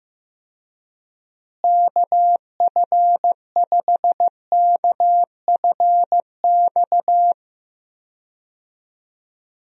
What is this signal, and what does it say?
Morse code, heard as a keyed tone, about 15 wpm, 700 Hz. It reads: KF5KFX
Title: USA Call Signs - 6 Characters - Repeated 3x - 15wpm